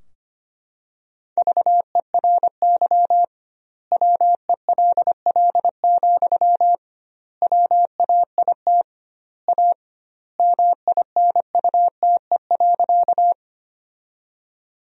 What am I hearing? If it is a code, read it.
Morse code, heard as a keyed tone, about 25 wpm, 700 Hz. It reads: VERY WELL, WAIT A MINUTE.
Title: 2x Word-Spacing - Sentences from Top 500 Words - 25wpm